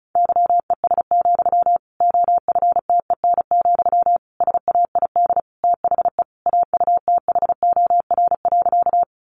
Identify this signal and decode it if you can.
Morse code, heard as a keyed tone, about 34 wpm, 700 Hz. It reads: YES, OFTEN, SAID THE AUTHOR.
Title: Sentences from Top 1200 Words 35wpm